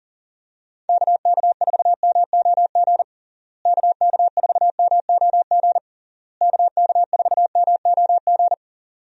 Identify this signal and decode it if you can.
Morse code, heard as a keyed tone, about 40 wpm, 700 Hz. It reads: KK4MOG KK4MOG KK4MOG
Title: USA Call Signs - 6 Characters - Repeated 3x - 40wpm